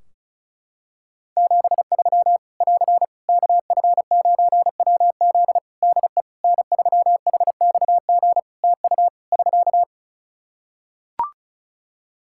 Morse code, heard as a keyed tone, about 36 wpm, 700 Hz. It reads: 73 <AR> KF9WZ DE N3HXG TU <SK>
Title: Practice QSO - Exchange 4 - Ending - 35wpm